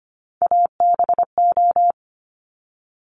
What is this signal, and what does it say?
Morse code, heard as a keyed tone, about 25 wpm, 700 Hz. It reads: ABO
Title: Top Three Letter Combinations in Words 25wpm